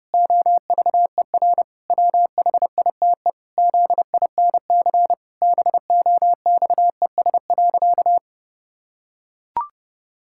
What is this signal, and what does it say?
Morse code, heard as a keyed tone, about 30 wpm, 700 Hz. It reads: OVER WHITE ZINC BOXES.
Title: Pangram Challenge v2 - 30wpm